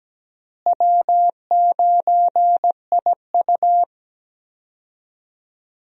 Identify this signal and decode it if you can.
Morse code, heard as a keyed tone, about 17 wpm, 700 Hz. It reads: W9IU
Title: USA Call Signs - 4 Characters - Repeated 2x - 17wpm